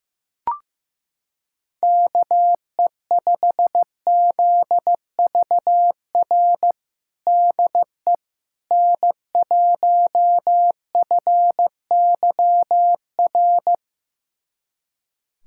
Morse code, heard as a keyed tone, about 15 wpm, 700 Hz. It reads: KE5ZVR DE N1FYR
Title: USA Callsign DE Callsign 15wpm